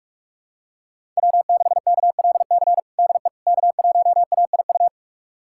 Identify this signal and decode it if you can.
Morse code, heard as a keyed tone, about 43 wpm, 700 Hz. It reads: W6KLC DE K1AIU
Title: USA Callsign DE Callsign - 5 Characters - 45wpm